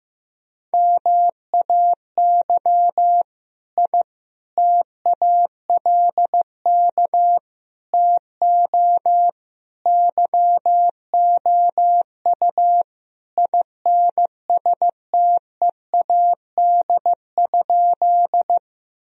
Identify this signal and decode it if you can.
Morse code, heard as a keyed tone, about 16 wpm, 700 Hz. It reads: MAY I TALK TO YOU INSTEAD?
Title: Sentences from Top 1000 Words - Review Part 4 - 15wpm